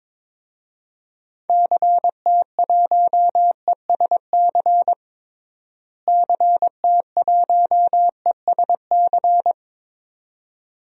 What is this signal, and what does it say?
Morse code, heard as a keyed tone, about 22 wpm, 700 Hz. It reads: CT1ESC CT1ESC
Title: DXCC-100 Call Signs - 6 Characters - Repeated 2x - 22wpm